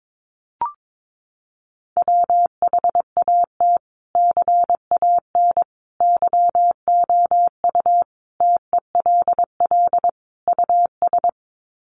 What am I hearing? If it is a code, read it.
Morse code, heard as a keyed tone, about 22 wpm, 700 Hz. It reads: WHAT CAN YOU TELL US
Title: Sets of 5 Words Encore 22wpm